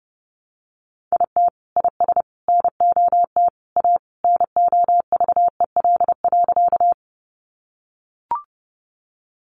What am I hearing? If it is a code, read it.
Morse code, heard as a keyed tone, about 30 wpm, 700 Hz. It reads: IT IS NOT A NOVEL.
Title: Sentences from Top 1300 Words 30wpm